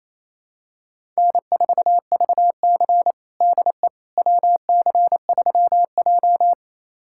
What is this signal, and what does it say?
Morse code, heard as a keyed tone, about 28 wpm, 700 Hz. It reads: N4VC DE WC3J
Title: USA Callsign DE Callsign - 4 Characters - 28wpm